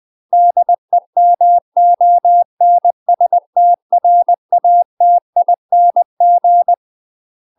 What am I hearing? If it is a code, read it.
Morse code, heard as a keyed tone, about 20 wpm, 700 Hz. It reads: DEMONSTRATING
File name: UK MASTER QSOs TRAINING / 20WPM / UK MASTER QSOs 0025